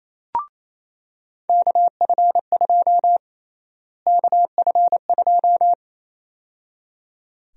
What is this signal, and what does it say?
Morse code, heard as a keyed tone, about 28 wpm, 700 Hz. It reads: KF2 KF2
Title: USA Call Sign Prefixes - Repeated 2x - 28wpm